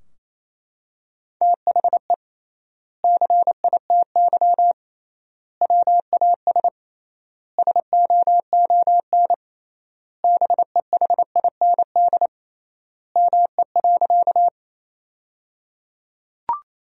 Morse code, heard as a keyed tone, about 28 wpm, 700 Hz. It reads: THE CITY WAS SOON BEHIND ME.
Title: 3x Word-Spacing - Sentences from Top 500 Words - 28wpm